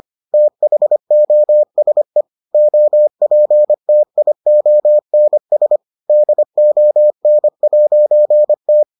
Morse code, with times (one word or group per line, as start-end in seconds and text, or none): 0.3-2.2 THOSE
2.5-5.8 OPTIONS
6.1-8.8 DON'T